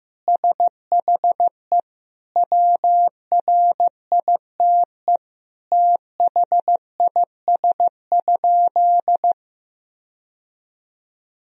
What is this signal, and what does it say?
Morse code, heard as a keyed tone, about 15 wpm, 700 Hz. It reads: SHE WRITE THIS?
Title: Sentences from Top 1000 Words - Review Part 1 - 15wpm